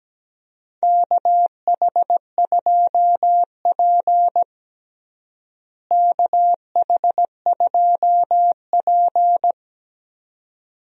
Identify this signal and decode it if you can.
Morse code, heard as a keyed tone, about 17 wpm, 700 Hz. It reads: KH2P KH2P
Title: USA Call Signs - 4 Characters - Repeated 2x - 17wpm